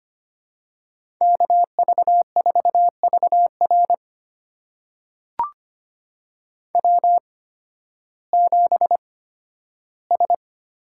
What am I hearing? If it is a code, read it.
Morse code, heard as a keyed tone, about 25 wpm, spaced as for 13 wpm, 700 Hz. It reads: KV4VR W7S
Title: USA Call Signs - 5 Characters - 8x Character Spacing - 25wpm